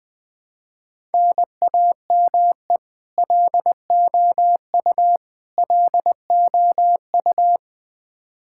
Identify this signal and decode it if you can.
Morse code, heard as a keyed tone, about 20 wpm, 700 Hz. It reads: NAME LOU LOU
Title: Practice QSO - Exchange 1 - Beginning - 20wpm